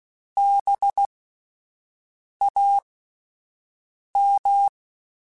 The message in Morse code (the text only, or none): BAM